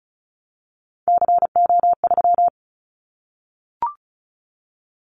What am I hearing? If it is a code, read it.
Morse code, heard as a keyed tone, about 34 wpm, 700 Hz. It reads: CO3
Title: Callsign Prefixes for Top 100 DXCC - 35wpm